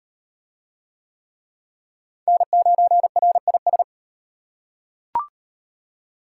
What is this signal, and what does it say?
Morse code, heard as a keyed tone, about 33 wpm, 700 Hz. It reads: N9RIS